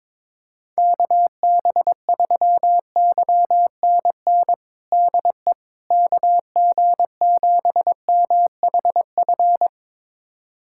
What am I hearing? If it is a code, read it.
Morse code, heard as a keyed tone, about 22 wpm, 700 Hz. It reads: KB3YNN DE KG7MHF
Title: USA Callsign DE Callsign - 6 Characters - 22wpm